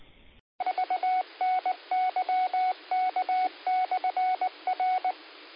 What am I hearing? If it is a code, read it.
Morse code, heard as a keyed tone, about 19 wpm, 740 Hz. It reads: VNYK/R
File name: qsorec SimplexLogic 20-40-38